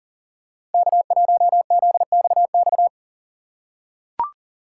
Morse code, heard as a keyed tone, about 38 wpm, 700 Hz. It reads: K1ZXX
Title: USA Call Signs - 5 Characters - 6x Character Spacing - 40wpm